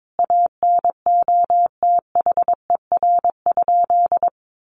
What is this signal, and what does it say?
Morse code, heard as a keyed tone, about 22 wpm, 700 Hz. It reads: ANOTHER?
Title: Sentences from Top 1400 Words 22wpm